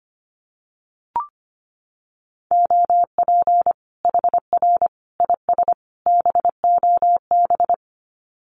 Morse code, heard as a keyed tone, about 25 wpm, 700 Hz. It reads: OP HR IS BOB
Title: Common QSO Phrases 25wpm